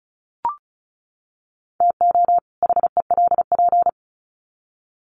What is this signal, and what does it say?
Morse code, heard as a keyed tone, about 34 wpm, 700 Hz. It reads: TO HELP
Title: Sets of 2 Words 35wpm